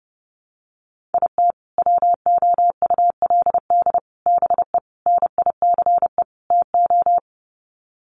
Morse code, heard as a keyed tone, about 28 wpm, 700 Hz. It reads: IT WOULD BE NICE TO
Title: Sets of 5 Words Encore2 30wpm